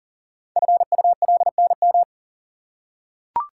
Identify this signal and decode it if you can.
Morse code, heard as a keyed tone, about 39 wpm, 700 Hz. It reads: FULNM
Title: Code Groups - 5 Letters - 40wpm